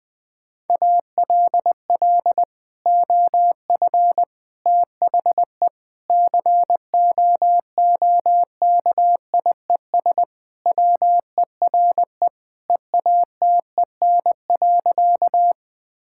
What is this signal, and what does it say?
Morse code, heard as a keyed tone, about 21 wpm, 700 Hz. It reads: ALL OF THE COOKIES WERE EATEN.
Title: Sentences with 2-Word Phrases - 20wpm